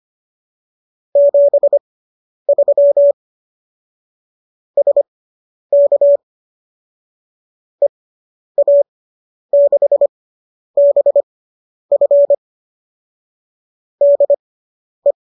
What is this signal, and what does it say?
Morse code, heard as a keyed tone, about 25 wpm, spaced as for 10 wpm, 570 Hz. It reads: 73 SK EA6BF DE